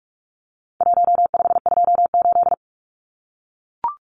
Morse code, heard as a keyed tone, about 44 wpm, 700 Hz. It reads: 1528
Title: Numbers - Contest Style - 4-Digit 45wpm